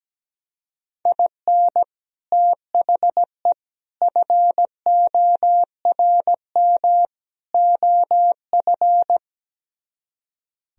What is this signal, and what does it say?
Morse code, heard as a keyed tone, about 17 wpm, 700 Hz. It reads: IN THE FORM OF
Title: Speed-Racing - Sets of 4 Words 17wpm